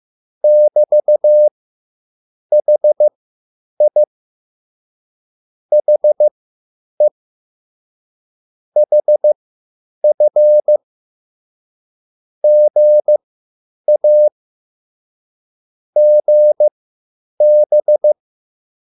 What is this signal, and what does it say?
Morse code, heard as a keyed tone, about 15 wpm, spaced as for 8 wpm, 600 Hz. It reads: <BT>HI HE HF GA GB
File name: QSL / CW lesson